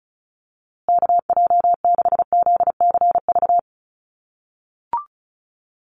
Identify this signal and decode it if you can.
Morse code, heard as a keyed tone, about 34 wpm, 700 Hz. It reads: KJ6ZCV
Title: USA Call Signs Repeated 3x 35wpm